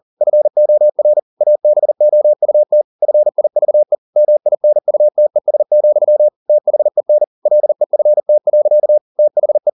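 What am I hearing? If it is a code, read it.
Morse code, heard as a keyed tone, about 40 wpm, 600 Hz. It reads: FOR ABOUT FIVE MINUTES, THEN LEFT. THE
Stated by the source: Morse code at 40 wpm, practice text, CW med Gnister.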